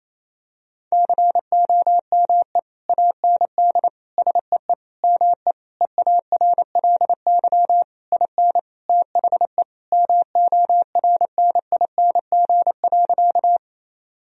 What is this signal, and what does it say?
Morse code, heard as a keyed tone, about 28 wpm, 700 Hz. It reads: COME AND SEE ME EARLY IN THE MORNING.
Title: Sentences from Top 1000 Words - Review Part 2 - 28wpm